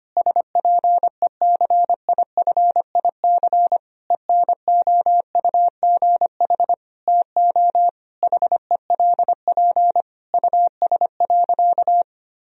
Morse code, heard as a keyed tone, about 25 wpm, 700 Hz. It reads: SPECIFIC ENOUGH TO HELP US.
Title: Sentences from Top 1000 Words - Review Part 3 - 25wpm